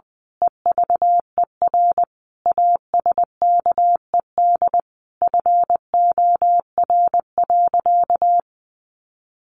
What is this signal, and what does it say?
Morse code, heard as a keyed tone, about 20 wpm, 700 Hz. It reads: EVER ASKED FOR.
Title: A Sound Of Thunder 20wpm